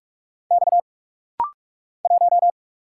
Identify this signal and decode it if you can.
Morse code, heard as a keyed tone, about 46 wpm, 700 Hz. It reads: X 1
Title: Single Letter-Number Alt - Rapid-Fire 45wpm